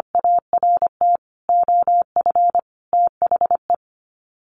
Morse code, heard as a keyed tone, about 25 wpm, 700 Hz. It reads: ART OF THE
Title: Sets of 5 Words 25wpm